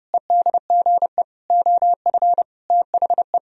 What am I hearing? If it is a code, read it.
Morse code, heard as a keyed tone, about 38 wpm, 700 Hz. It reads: EDGE OF THE